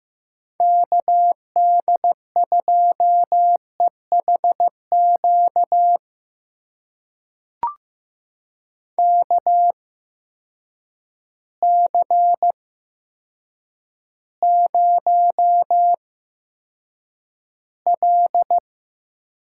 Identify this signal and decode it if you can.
Morse code, heard as a keyed tone, about 15 wpm, spaced as for 8 wpm, 700 Hz. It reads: KD2EHQ KC0L
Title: USA Call Signs - 6 Characters - 8x Character Spacing - 15wpm